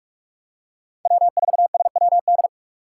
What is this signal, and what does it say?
Morse code, heard as a keyed tone, about 40 wpm, 700 Hz. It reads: W4SWD